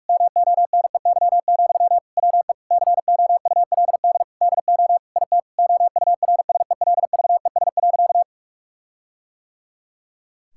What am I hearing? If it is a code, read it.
Morse code, heard as a keyed tone, about 42 wpm, 700 Hz. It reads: MONEY, WE COULD DO IT OURSELVES.
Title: Sentences from Top 1600 Words 45wpm